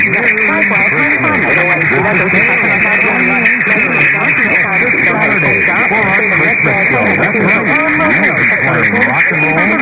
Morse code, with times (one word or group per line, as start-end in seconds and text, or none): none